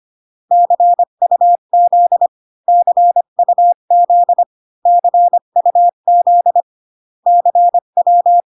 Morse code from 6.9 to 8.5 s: CW